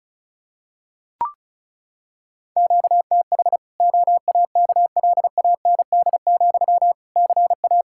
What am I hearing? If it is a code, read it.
Morse code, heard as a keyed tone, about 36 wpm, 700 Hz. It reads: QTH OAKLAND, CA
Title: Practice QSO - Exchange 1 - Beginning - 35wpm